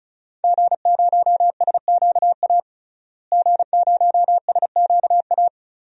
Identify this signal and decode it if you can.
Morse code, heard as a keyed tone, about 34 wpm, 700 Hz. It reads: G0SQA G0SQA
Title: DXCC-100 Call Signs - 5 Characters - Repeated 2x - 35wpm